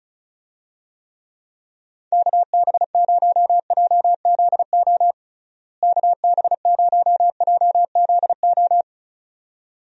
Morse code, heard as a keyed tone, about 36 wpm, 700 Hz. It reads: KB0JZO KB0JZO
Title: USA Call Signs - 6 Characters - Repeated 2x - 35wpm